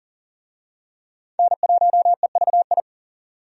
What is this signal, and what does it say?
Morse code, heard as a keyed tone, about 40 wpm, 700 Hz. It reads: N1EVI
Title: USA Call Signs - 5 Characters - 40wpm